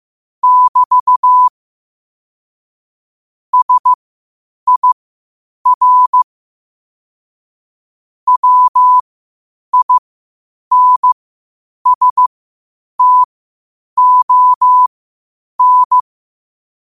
Morse code, heard as a keyed tone, about 15 wpm, spaced as for 8 wpm, 1000 Hz. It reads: <BT> SIR WINSTON
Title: 8 WPM morse code quotes for Fri, 01 Mar 2024